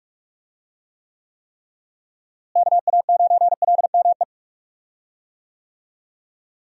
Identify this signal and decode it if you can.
Morse code, heard as a keyed tone, about 44 wpm, 700 Hz. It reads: KA9LME
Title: USA Call Signs - 6 Characters - Repeated 3x - 45wpm